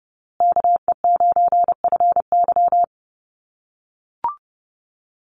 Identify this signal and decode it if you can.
Morse code, heard as a keyed tone, about 30 wpm, 700 Hz. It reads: KE9FY
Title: USA Call Signs Repeated 2x 30wpm